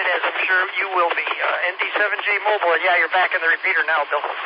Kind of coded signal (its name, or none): Emergency Alert System (SAME) tones